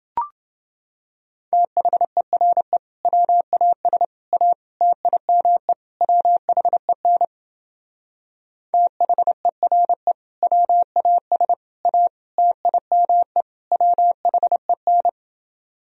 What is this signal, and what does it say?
Morse code, heard as a keyed tone, about 28 wpm, 700 Hz. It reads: THERE WAS A TIME WHEN THERE WAS A TIME WHEN